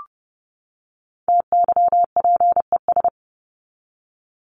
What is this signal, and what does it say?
Morse code, heard as a keyed tone, about 30 wpm, 700 Hz. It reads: TYPES
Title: ES:S Suffix - Top 100 Words with Base Words 30wpm